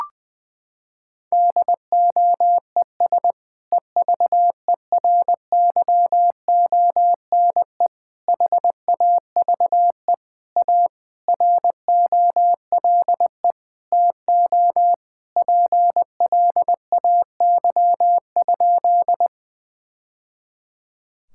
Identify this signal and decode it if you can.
Morse code, heard as a keyed tone, about 21 wpm, 700 Hz. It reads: DOES EVERYONE HAVE A ROLE TO PLAY?
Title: Sentences from Top 1000 Words - Review Part 6 - 20wpm